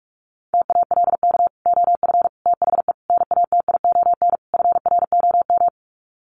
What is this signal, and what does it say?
Morse code, heard as a keyed tone, about 44 wpm, 700 Hz. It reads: TALK OF THE NATION FROM
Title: Sets of 5 Words 45wpm